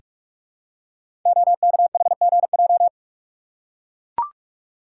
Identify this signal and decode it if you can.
Morse code, heard as a keyed tone, about 44 wpm, 700 Hz. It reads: OKHGJ
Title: Code Groups - 5 Letter:Numbers - 45wpm